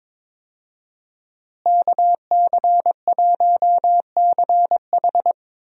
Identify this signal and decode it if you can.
Morse code, heard as a keyed tone, about 22 wpm, 700 Hz. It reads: KC1CH
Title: USA Call Signs - 5 Characters - Repeated 3x - 22wpm